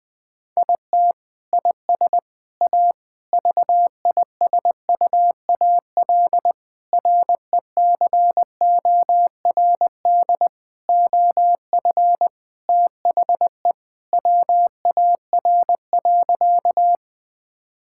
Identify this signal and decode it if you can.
Morse code, heard as a keyed tone, about 20 wpm, 700 Hz. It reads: IT IS A VISUAL RECORD OF THE WAR.
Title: Sentences from Top 1000 Words - Review Part 6 - 20wpm